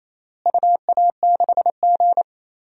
Morse code, heard as a keyed tone, about 28 wpm, 700 Hz. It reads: UA6G
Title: DXCC-100 Call Signs - 4 Characters - Repeated 3x - 28wpm